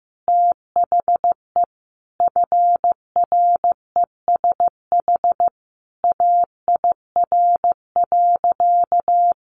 Morse code, heard as a keyed tone, about 15 wpm, 700 Hz. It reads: THE FRESH AIR.
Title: Sentences from Top 1200 Words 15wpm